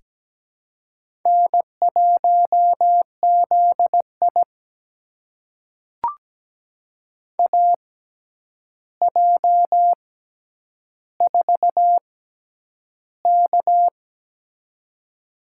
Morse code, heard as a keyed tone, about 17 wpm, spaced as for 9 wpm, 700 Hz. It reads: N1ZI AJ4K
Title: USA Call Signs - 4 Characters - 6x Character Spacing - 17wpm